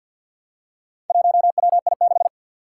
Morse code, heard as a keyed tone, about 51 wpm, 700 Hz. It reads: 1WI6
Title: Four Challenging Letter-Number Combinations 50wpm